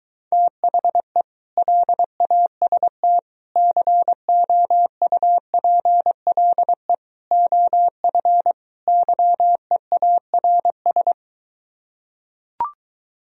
Morse code, heard as a keyed tone, about 23 wpm, 700 Hz. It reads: THE LAST COUPLE OF YEARS